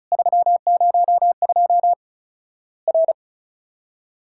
Morse code, heard as a keyed tone, about 36 wpm, 700 Hz. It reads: 302 R
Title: Worked All Europe DX Contest - QTC Exchange - 35wpm